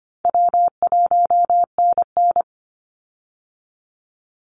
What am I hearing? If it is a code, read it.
Morse code, heard as a keyed tone, about 25 wpm, 700 Hz. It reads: W1NN
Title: Easy Call Signs Repeated 3x - 25wpm